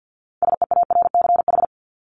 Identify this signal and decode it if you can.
Morse code, heard as a keyed tone, about 50 wpm, 700 Hz. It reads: SEARCH
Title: Top 501-750 Words 50wpm